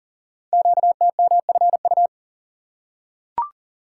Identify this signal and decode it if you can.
Morse code, heard as a keyed tone, about 38 wpm, 700 Hz. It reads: QTMFU
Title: Code Groups - 5 Letters - 40wpm